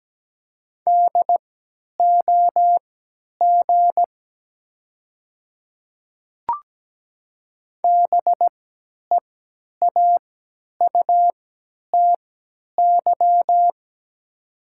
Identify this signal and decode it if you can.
Morse code, heard as a keyed tone, about 17 wpm, spaced as for 10 wpm, 700 Hz. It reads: DOG BEAUTY
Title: 3x Character Spacing - Top 500 Words - 17wpm